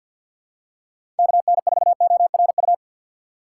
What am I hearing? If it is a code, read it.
Morse code, heard as a keyed tone, about 49 wpm, 700 Hz. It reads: KN4ORU
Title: USA Call Signs - 6 Characters - 6x Character Spacing - 50wpm